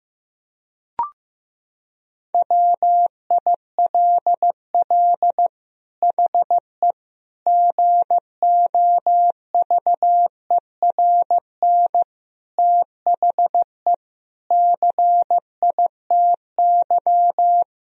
Morse code, heard as a keyed tone, about 16 wpm, 700 Hz. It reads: WILL HE GOVERN THE CITY
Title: Sentences from Top 1000 Words - Review Part 2 - 15wpm